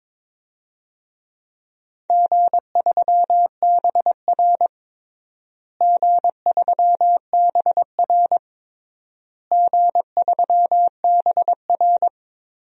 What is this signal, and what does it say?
Morse code, heard as a keyed tone, about 22 wpm, 700 Hz. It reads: G3BR G3BR G3BR
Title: DXCC-100 Call Signs - 4 Characters - Repeated 3x - 22wpm